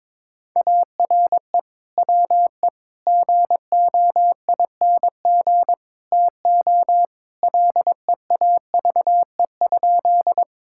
Morse code, heard as a keyed tone, about 22 wpm, 700 Hz. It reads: ARE WE GOING TO LEAVE?
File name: practice / 22wpm / Sentences with 4-Word Phrases - 22wpm